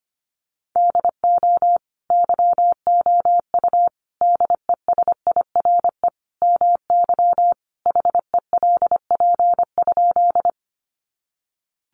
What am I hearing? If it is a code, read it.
Morse code, heard as a keyed tone, about 25 wpm, 700 Hz. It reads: DO YOU DESIRE MY HELP?
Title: Sentences from Top 1100 Words 25wpm